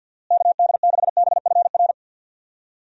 Morse code, heard as a keyed tone, about 47 wpm, 700 Hz. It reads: KD6BFR
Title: USA Call Signs - 6 Characters - Repeated 2x - 50wpm